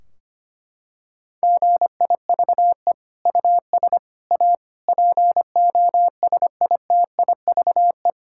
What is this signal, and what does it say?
Morse code, heard as a keyed tone, about 25 wpm, 700 Hz. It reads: GIVE US A POSITIVE
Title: Sentences from Top 1600 Words 25wpm